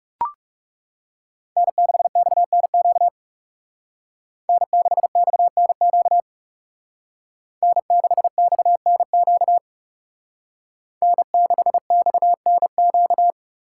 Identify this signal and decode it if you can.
Morse code, heard as a keyed tone, about 36 wpm, 700 Hz. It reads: N6XNQ N6XNQ N6XNQ N6XNQ
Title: Speed Racing - USA Call Signs 30wpm